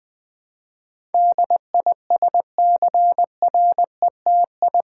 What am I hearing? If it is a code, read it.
Morse code, heard as a keyed tone, about 20 wpm, 700 Hz. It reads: DISCRETI
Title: Build A Word - 4 Letters At A Time - 10-Letter Words - 20wpm